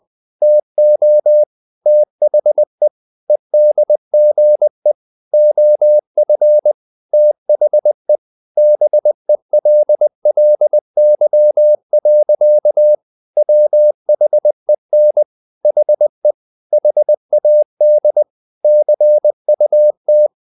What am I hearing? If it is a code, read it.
Morse code, heard as a keyed tone, about 20 wpm, 600 Hz. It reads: TO THE EDGE OF THE BELLY. WHEN HE HAD CUT
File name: the old man and the sea / the old man and the sea part 1 20wpm